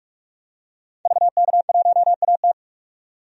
Morse code, heard as a keyed tone, about 48 wpm, 700 Hz. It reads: VK1AT
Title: DXCC-100 Call Signs - 5 Characters - 45wpm